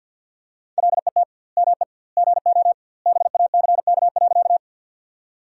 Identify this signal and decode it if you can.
Morse code, heard as a keyed tone, about 50 wpm, 700 Hz. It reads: LET ME GO BACK.
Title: 2x Word-Spacing - Sentences from Top 300 Words - 50wpm